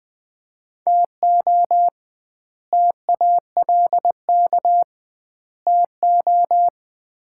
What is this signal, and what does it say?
Morse code, heard as a keyed tone, about 20 wpm, 700 Hz. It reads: TO TALK TO
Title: 2x Word-Spacing - Sets of 3 Words - 20wpm